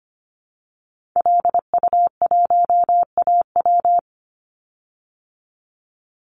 Morse code, heard as a keyed tone, about 25 wpm, 700 Hz. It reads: LU1AW
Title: Call Signs Repeated 2x - 25wpm